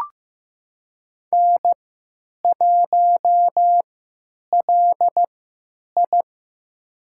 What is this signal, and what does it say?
Morse code, heard as a keyed tone, about 15 wpm, spaced as for 9 wpm, 700 Hz. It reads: N1LI
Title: USA Call Signs - 4 Characters - 3x Character Spacing - 15wpm